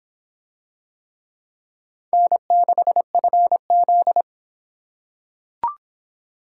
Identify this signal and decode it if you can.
Morse code, heard as a keyed tone, about 25 wpm, 700 Hz. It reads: N6FZ